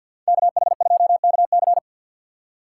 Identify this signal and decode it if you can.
Morse code, heard as a keyed tone, about 51 wpm, 700 Hz. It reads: KH2KC
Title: USA Call Signs - 2x Character Spacing - 50wpm